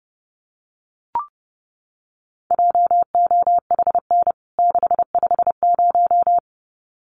Morse code, heard as a keyed tone, about 28 wpm, 700 Hz. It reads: JOHN 650
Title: CWT Exchange 30wpm